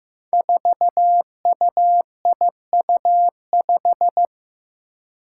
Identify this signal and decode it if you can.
Morse code, heard as a keyed tone, about 15 wpm, 700 Hz. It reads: 4UIU5
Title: Five Challenging Letter-Number Combinations 15wpm